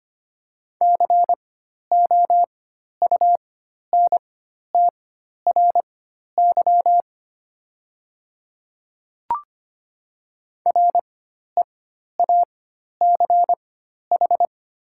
Morse code, heard as a keyed tone, about 25 wpm, spaced as for 12 wpm, 690 Hz. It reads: COUNTRY REACH
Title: 4x Character Spacing - Top 500 Words - 25wpm